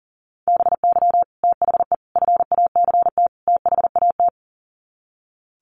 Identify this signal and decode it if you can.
Morse code, heard as a keyed tone, about 40 wpm, 700 Hz. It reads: BY THE FACT THAT
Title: Sets of 4 Words 40wpm